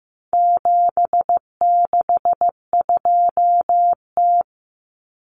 Morse code, heard as a keyed tone, about 15 wpm, 700 Hz. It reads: 762T
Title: Common Cut Numbers - 4-Digit 15wpm